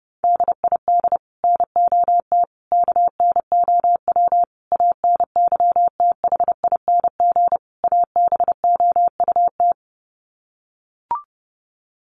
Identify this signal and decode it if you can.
Morse code, heard as a keyed tone, about 30 wpm, 700 Hz. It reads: DID NOT KNOW ANYTHING ABOUT
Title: Sets of 5 Words Encore2 30wpm